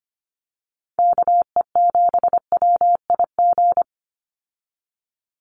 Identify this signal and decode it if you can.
Morse code, heard as a keyed tone, about 25 wpm, 700 Hz. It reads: KE7WIG
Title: USA Call Signs 25wpm